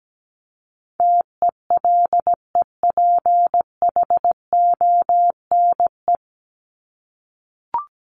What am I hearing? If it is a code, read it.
Morse code, heard as a keyed tone, about 17 wpm, 700 Hz. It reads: TELEPHONE
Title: Top 1501-1750 Words 17wpm